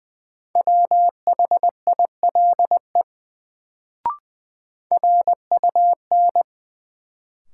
Morse code, heard as a keyed tone, about 20 wpm, 700 Hz. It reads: WHILE RUN
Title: Sentences from Top 300 Words - New Word Drill - 20wpm